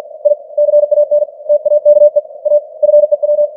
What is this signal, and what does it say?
Morse code, heard as a keyed tone, about 51 wpm, 590 Hz. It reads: I CAN TAKE A FEW
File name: archived / CWBros